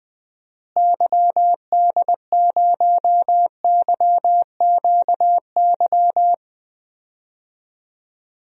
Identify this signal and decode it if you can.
Morse code, heard as a keyed tone, about 20 wpm, 700 Hz. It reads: YD0YQY
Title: DXCC-100 Call Signs - 6 Characters - Repeated 2x - 20wpm